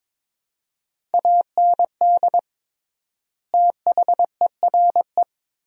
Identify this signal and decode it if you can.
Morse code, heard as a keyed tone, about 22 wpm, 700 Hz. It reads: AND THERE
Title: 3x Word-Spacing - Sentences from Top 300 Words - 22wpm